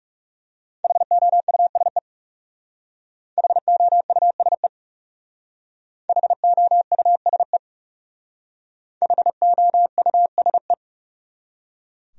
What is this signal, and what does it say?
Morse code, heard as a keyed tone, about 39 wpm, 700 Hz. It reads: HOUSE HOUSE HOUSE HOUSE